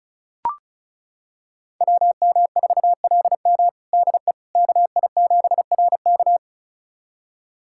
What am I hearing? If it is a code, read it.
Morse code, heard as a keyed tone, about 36 wpm, 700 Hz. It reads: WM4LM DE KI7RK
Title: USA Callsign DE Callsign - 5 Characters - 35wpm